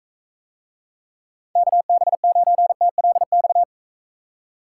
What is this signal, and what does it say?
Morse code, heard as a keyed tone, about 44 wpm, 700 Hz. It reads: KB9TLX